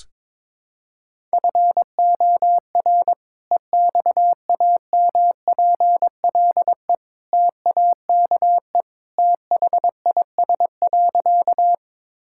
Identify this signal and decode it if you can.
Morse code, heard as a keyed tone, about 21 wpm, 700 Hz. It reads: FOR EXAMPLE TAKE THIS.
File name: practice / 22wpm / Sentences with 2-Word Phrases - 22wpm